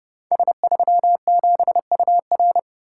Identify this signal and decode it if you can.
Morse code, heard as a keyed tone, about 30 wpm, 700 Hz. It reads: S37UR
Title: Code Groups - 5 Letter:Numbers - 30wpm